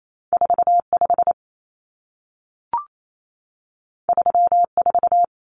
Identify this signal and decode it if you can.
Morse code, heard as a keyed tone, about 28 wpm, 700 Hz. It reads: 45 34
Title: Numbers - 2-Digit 28wpm